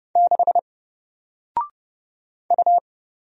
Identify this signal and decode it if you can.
Morse code, heard as a keyed tone, about 30 wpm, 700 Hz. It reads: 6 U
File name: Single-Letter-Number-30wpm